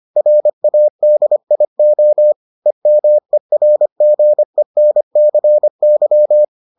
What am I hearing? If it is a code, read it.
Morse code, heard as a keyed tone, about 25 wpm, 600 Hz. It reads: RADIO EMERGENCY